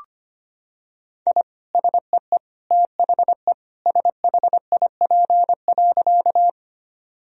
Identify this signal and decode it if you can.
Morse code, heard as a keyed tone, about 25 wpm, 700 Hz. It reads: I SEE THE SHIP.
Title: Sentences from Top 1000 Words - Review Part 2 - 25wpm